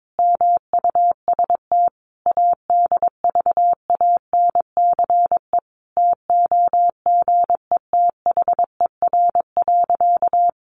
Morse code, heard as a keyed tone, about 22 wpm, 700 Hz. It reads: MUST ADVANCE TOGETHER.
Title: Sentences from Top 1400 Words 22wpm